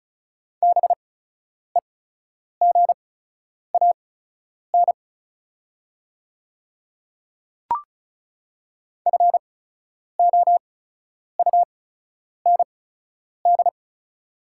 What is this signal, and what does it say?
Morse code, heard as a keyed tone, about 36 wpm, spaced as for 10 wpm, 690 Hz. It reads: BEGAN FOUND
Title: 8x Character Spacing - Top 500 Words - 35wpm